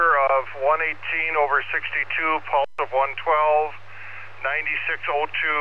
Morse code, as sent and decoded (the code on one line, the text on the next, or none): none